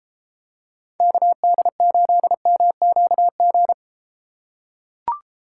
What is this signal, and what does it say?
Morse code, heard as a keyed tone, about 31 wpm, 700 Hz. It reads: KD8MQG